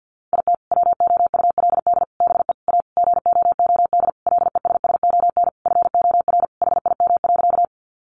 Morse code, heard as a keyed tone, about 50 wpm, 700 Hz. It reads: IT WOULD BE A GOOD LESSON FOR HIM.
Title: Sentences from Top 1200 Words 50wpm